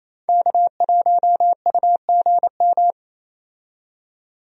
Morse code, heard as a keyed tone, about 27 wpm, 700 Hz. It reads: K1UGM
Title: USA Call Signs - 5 Characters - 28wpm